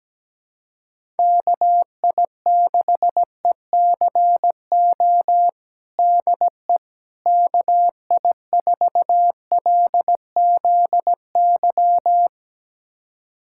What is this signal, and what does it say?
Morse code, heard as a keyed tone, about 17 wpm, 700 Hz. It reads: KI6ECO DE KI4LZY
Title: USA Callsign DE Callsign - 6 Characters - 17wpm